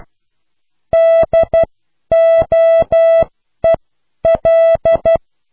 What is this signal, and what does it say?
Morse code, heard as a keyed tone, about 12 wpm, spaced as for 10 wpm, 660 Hz. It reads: DOEL